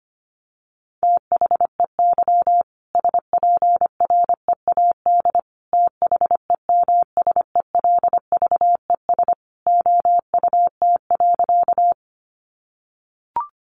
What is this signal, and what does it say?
Morse code, heard as a keyed tone, about 25 wpm, 700 Hz. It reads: THEY SPREAD THEMSELVES OUT.
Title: Sentences from Top 1200 Words 25wpm